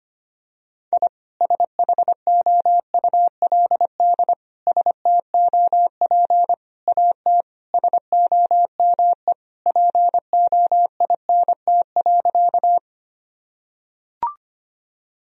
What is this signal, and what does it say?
Morse code, heard as a keyed tone, about 25 wpm, 700 Hz. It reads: I SHOULD STOP AT SOME POINT.
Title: Sentences from Top 1000 Words - Review Part 1 - 25wpm